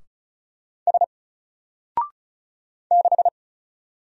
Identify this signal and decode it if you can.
Morse code, heard as a keyed tone, about 34 wpm, 700 Hz. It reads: S 6
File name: Single-Letter-Number-35wpm